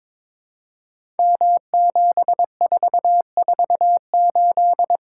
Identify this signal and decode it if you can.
Morse code, heard as a keyed tone, about 22 wpm, 700 Hz. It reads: M7448
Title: Code Groups - 5 Letter:Numbers - 22wpm